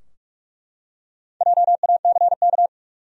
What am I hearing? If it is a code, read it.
Morse code, heard as a keyed tone, about 42 wpm, 700 Hz. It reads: JACK
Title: USA Common Names - Max 5 Letters - 45wpm